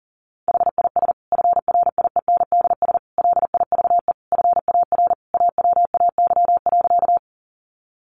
Morse code, heard as a keyed tone, about 40 wpm, 700 Hz. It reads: HIS FRIENDS LIVE FAR AWAY.
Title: Additional Easy Phrases 40wpm